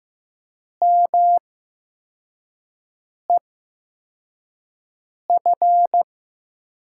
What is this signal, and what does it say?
Morse code, heard as a keyed tone, about 15 wpm, spaced as for 4 wpm, 700 Hz. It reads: MEF
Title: USA Call Signs - 8x Character Spacing - 15wpm